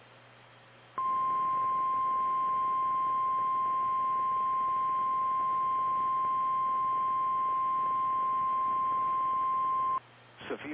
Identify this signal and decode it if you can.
Emergency Alert System (SAME) tones (the 1050 Hz attention tone), with speech. It is heard from 1.0 to 10.0 s.
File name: eas / KKPRKXPN / KKPRKXPN